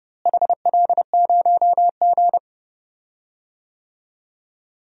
Morse code, heard as a keyed tone, about 30 wpm, 700 Hz. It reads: HL0G
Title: DXCC-100 Call Signs - 4 Characters - Repeated 3x - 30wpm